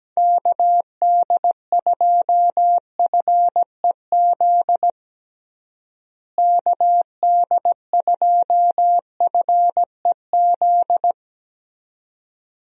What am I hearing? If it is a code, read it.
Morse code, heard as a keyed tone, about 17 wpm, 700 Hz. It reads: KD2FEZ KD2FEZ
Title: USA Call Signs - 6 Characters - Repeated 2x - 17wpm